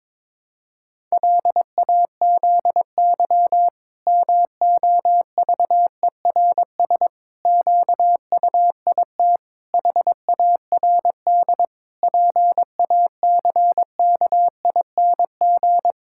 Morse code, heard as a keyed tone, about 22 wpm, 700 Hz. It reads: LAZY MOVERS QUIT HARD PACKING
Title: Pangram Challenge v2 - 22wpm